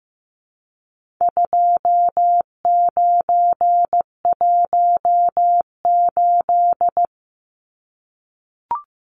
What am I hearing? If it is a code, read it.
Morse code, heard as a keyed tone, about 15 wpm, 700 Hz. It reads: 2918
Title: Numbers - 4-Digit 15wpm